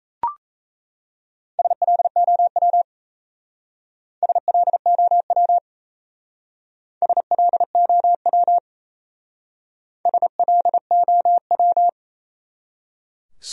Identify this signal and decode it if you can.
Morse code, heard as a keyed tone, about 35 wpm, 700 Hz. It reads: SLOW SLOW SLOW SLOW
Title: Speed-Racing Top 500 Words 28wpm